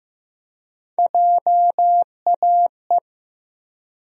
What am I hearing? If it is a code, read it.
Morse code, heard as a keyed tone, about 15 wpm, 700 Hz. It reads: JAE
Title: USA Call Sign Postfixes - 3 Letters 15wpm